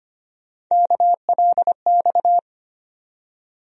Morse code, heard as a keyed tone, about 25 wpm, 700 Hz. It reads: KLX
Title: USA Call Sign Postfixes - 3 Letters 25wpm